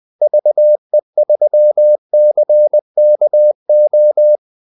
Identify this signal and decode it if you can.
Morse code, heard as a keyed tone, about 20 wpm, 600 Hz. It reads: VE3CKO
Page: Kaldesignaler 20 wpm